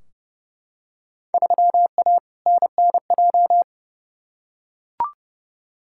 Morse code, heard as a keyed tone, about 30 wpm, 700 Hz. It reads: 3A NNJ
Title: Field Day Exchanges Encore - 30wpm